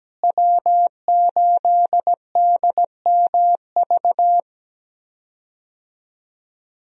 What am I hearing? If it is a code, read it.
Morse code, heard as a keyed tone, about 17 wpm, 700 Hz. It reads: W8DMV
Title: USA Call Signs - 5 Characters - Repeated 2x - 17wpm